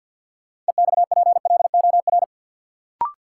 Morse code, heard as a keyed tone, about 52 wpm, 700 Hz. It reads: EXPLOR
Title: Sentences from Top 1100 Words - New Word Drill - 50wpm